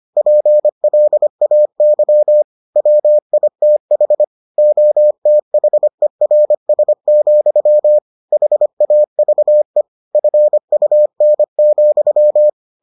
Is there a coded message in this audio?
Yes: Morse code from 0.2 to 12.5 s, heard as a keyed tone, about 25 wpm, 600 Hz. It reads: PLAY WITH OTHERS, HAVE FUN,